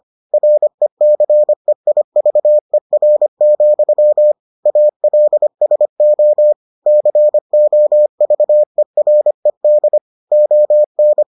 Morse code, heard as a keyed tone, about 25 wpm, 600 Hz. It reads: RECEIVER, ALSO COVERED ON